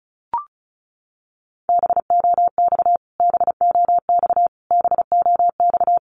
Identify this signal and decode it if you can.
Morse code, heard as a keyed tone, about 34 wpm, 700 Hz. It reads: BOX BOX BOX
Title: Top 500 Words Repeated 35wpm